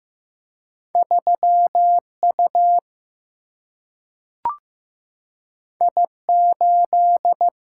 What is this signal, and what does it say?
Morse code, heard as a keyed tone, about 15 wpm, 700 Hz. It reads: 3U I8
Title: Single-Letter Single-Number 15wpm